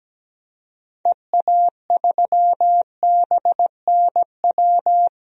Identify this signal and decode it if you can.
Morse code, heard as a keyed tone, about 17 wpm, 700 Hz. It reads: EA3BNW
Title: DXCC-100 Call Signs - 6 Characters - 17wpm